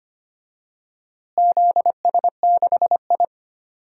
Morse code, heard as a keyed tone, about 25 wpm, 700 Hz. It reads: ZS6I
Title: DXCC-100 Call Signs - 4 Characters - Repeated 2x - 25wpm